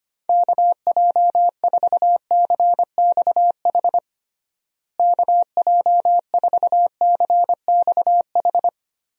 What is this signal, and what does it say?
Morse code, heard as a keyed tone, about 25 wpm, 700 Hz. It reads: KJ4CXH KJ4CXH
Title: USA Call Signs - 6 Characters - Repeated 3x - 25wpm